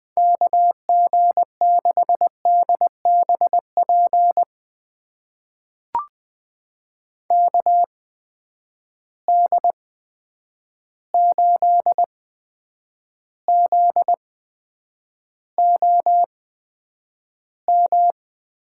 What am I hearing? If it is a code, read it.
Morse code, heard as a keyed tone, about 20 wpm, spaced as for 9 wpm, 700 Hz. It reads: KG6DBP KD8ZOM
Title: USA Call Signs - 6 Characters - 8x Character Spacing - 20wpm